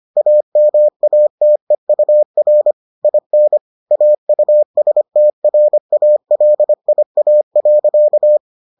Morse code, heard as a keyed tone, about 25 wpm, 600 Hz. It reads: AMATEUR IN AUSTRALIA.